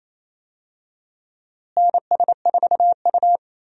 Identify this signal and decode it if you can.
Morse code, heard as a keyed tone, about 28 wpm, 700 Hz. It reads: NS4U
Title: USA Call Signs - 4 Characters - Repeated 3x - 28wpm